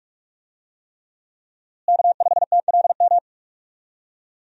Morse code, heard as a keyed tone, about 41 wpm, 700 Hz. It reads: K5TLM
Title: USA Call Signs - 5 Characters - Repeated 3x - 45wpm